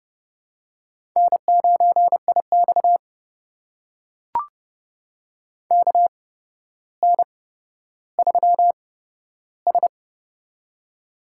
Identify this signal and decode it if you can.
Morse code, heard as a keyed tone, about 30 wpm, spaced as for 13 wpm, 700 Hz. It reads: N9IX KN3S
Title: USA Call Signs - 4 Characters - 8x Character Spacing - 30wpm